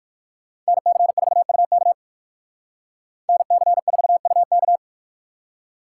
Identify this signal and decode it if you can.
Morse code, heard as a keyed tone, about 58 wpm, 700 Hz. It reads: NC4UK NC4UK